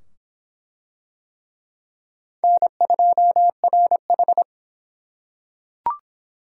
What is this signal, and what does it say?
Morse code, heard as a keyed tone, about 23 wpm, 700 Hz. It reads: N2RH